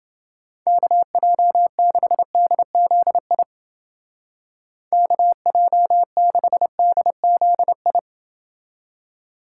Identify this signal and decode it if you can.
Morse code, heard as a keyed tone, about 29 wpm, 700 Hz. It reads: KJ6DZI KJ6DZI